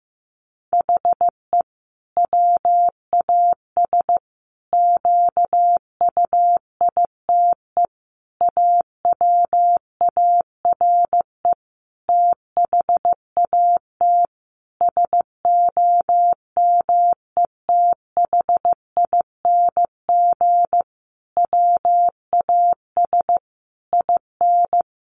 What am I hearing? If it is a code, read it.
Morse code, heard as a keyed tone, about 15 wpm, 700 Hz. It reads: HE WAS QUITE AWARE THAT SOMETHING WAS IN
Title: Sentences from Top 1400 Words 15wpm